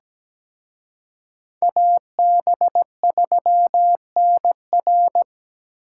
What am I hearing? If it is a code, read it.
Morse code, heard as a keyed tone, about 20 wpm, 700 Hz. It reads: AB3NR
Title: Speed Racing - USA Call Signs 17wpm